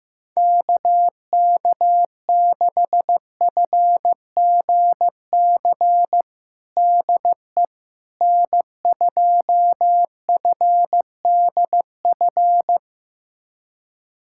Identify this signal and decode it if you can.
Morse code, heard as a keyed tone, about 15 wpm, 700 Hz. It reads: KK6FGC DE N2FDF
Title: USA Callsign DE Callsign 15wpm